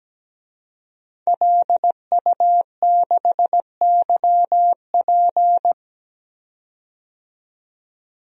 Morse code, heard as a keyed tone, about 17 wpm, 700 Hz. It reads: LU6YP
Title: DXCC-100 Call Signs - 5 Characters - 17wpm